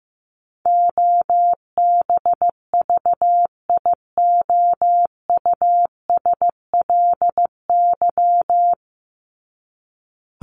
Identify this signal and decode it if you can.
Morse code, heard as a keyed tone, about 14 wpm, 700 Hz. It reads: OBVIOUSLY
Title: LY Suffix - Top 100 Words with Base Words 15wpm